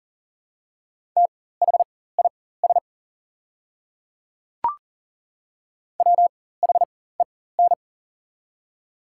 Morse code, heard as a keyed tone, about 40 wpm, spaced as for 19 wpm, 700 Hz. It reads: THIS WHEN
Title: 4x Character Spacing - Top 200 Words - 40wpm